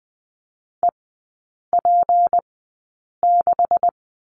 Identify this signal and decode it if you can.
Morse code, heard as a keyed tone, about 20 wpm, spaced as for 8 wpm, 700 Hz. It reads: EP6
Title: 205 Lesson 33e - Letter j - Three Characters 20wpm